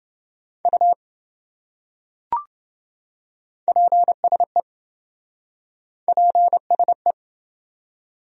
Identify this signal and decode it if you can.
Morse code, heard as a keyed tone, about 28 wpm, 700 Hz. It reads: U PSE PSE